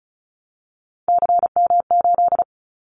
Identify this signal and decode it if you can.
Morse code, heard as a keyed tone, about 34 wpm, 700 Hz. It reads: CM8
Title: Callsign Prefixes for Top 100 DXCC - 35wpm